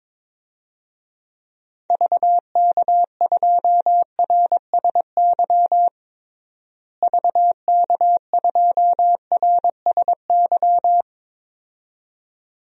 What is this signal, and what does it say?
Morse code, heard as a keyed tone, about 22 wpm, 700 Hz. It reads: VK2RSY VK2RSY
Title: DXCC-100 Call Signs - 6 Characters - Repeated 2x - 22wpm